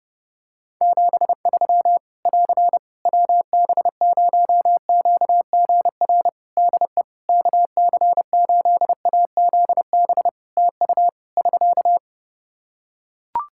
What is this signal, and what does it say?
Morse code, heard as a keyed tone, about 32 wpm, 700 Hz. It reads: 73 <AR> WB0QGR DE KC8AZB TU <SK>
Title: Practice QSO - Exchange 4 - Ending - 30wpm